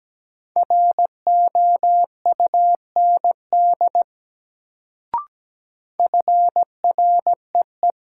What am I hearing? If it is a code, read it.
Morse code, heard as a keyed tone, about 17 wpm, 700 Hz. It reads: ROUND FREE
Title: Sets of 5 Words Encore - New Word Drill - 17wpm